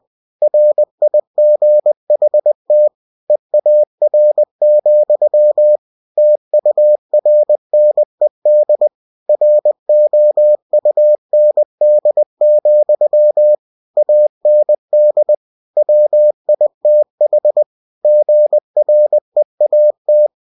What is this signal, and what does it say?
Morse code, heard as a keyed tone, about 20 wpm, 600 Hz. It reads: RIGHT EAR, TURNED ROUND, AND WITH GREAT